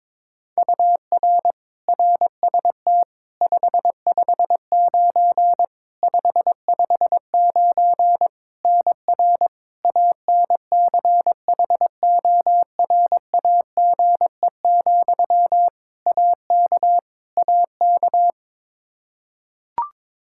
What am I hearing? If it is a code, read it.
Morse code, heard as a keyed tone, about 22 wpm, 700 Hz. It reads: UR RST 559 559 NR ANCHORAGE, AK AK
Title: Practice QSO - Exchange 1 - Beginning - 22wpm